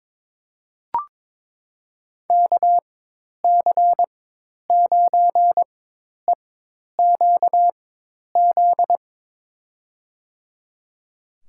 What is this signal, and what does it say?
Morse code, heard as a keyed tone, about 22 wpm, spaced as for 10 wpm, 700 Hz. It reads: KC9EQZ
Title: USA Call Signs - 6 Characters - 4x Character Spacing - 22wpm